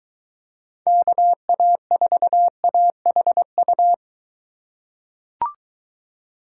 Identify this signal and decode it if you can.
Morse code, heard as a keyed tone, about 21 wpm, 700 Hz. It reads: KA4AHU